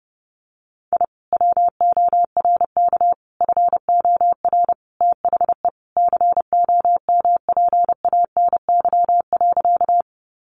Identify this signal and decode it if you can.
Morse code, heard as a keyed tone, about 30 wpm, 700 Hz. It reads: I WORK FOR THE COMPANY.
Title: Sentences from Top 600 Words 30wpm